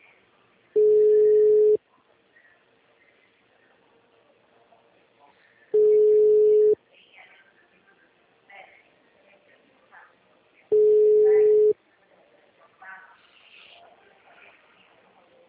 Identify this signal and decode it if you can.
Ringback tone (Europe, 425 Hz), with speech.